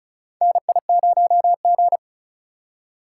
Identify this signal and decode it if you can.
Morse code, heard as a keyed tone, about 36 wpm, 700 Hz. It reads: NI0G
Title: USA Call Signs - 4 Characters - 8x Character Spacing - 35wpm